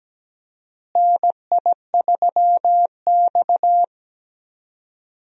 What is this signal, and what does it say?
Morse code, heard as a keyed tone, about 17 wpm, 700 Hz. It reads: NI3X
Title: USA Call Signs - 4 Characters - Repeated 2x - 17wpm